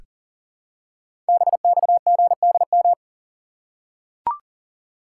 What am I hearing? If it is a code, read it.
Morse code, heard as a keyed tone, about 44 wpm, 700 Hz. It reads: BXGDM
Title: Code Groups - 5 Letters - 40wpm